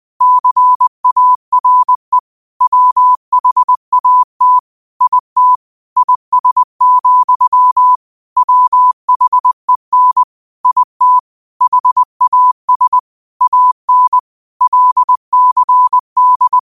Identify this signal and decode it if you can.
Morse code, heard as a keyed tone, about 20 wpm, 1000 Hz. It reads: CARE WHAT IT IS, WHEN IT HAS AN LCD